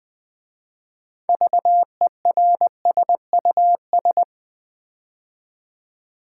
Morse code, heard as a keyed tone, about 20 wpm, 700 Hz. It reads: VERSUS
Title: 6 Letter Words - First 3 Letters Given - 20wpm